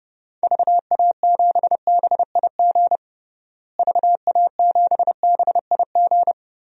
Morse code, heard as a keyed tone, about 32 wpm, 700 Hz. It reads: VA7BIG VA7BIG
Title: DXCC-100 Call Signs - 6 Characters - Repeated 3x - 30wpm